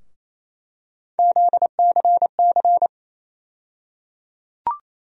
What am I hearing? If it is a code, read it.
Morse code, heard as a keyed tone, about 28 wpm, 700 Hz. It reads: ZCC
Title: USA Call Sign Postfixes - 3 Letters 28wpm